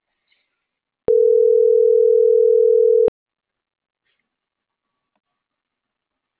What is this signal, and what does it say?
Ringback tone (North America), 440 + 480 Hz.